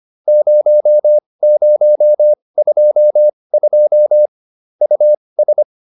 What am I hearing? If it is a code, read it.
Morse code, heard as a keyed tone, about 25 wpm, spaced as for 20 wpm, 600 Hz. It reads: 0022 US